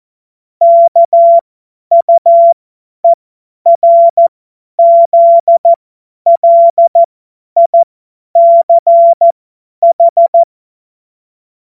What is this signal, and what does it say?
Morse code, heard as a keyed tone, about 14 wpm, spaced as for 10 wpm, 680 Hz. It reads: KUERZLICH
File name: Morsekurs / Witze / Witz 16. Mai 2023 50 F70